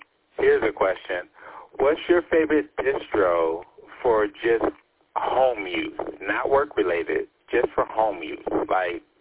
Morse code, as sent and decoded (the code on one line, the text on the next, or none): none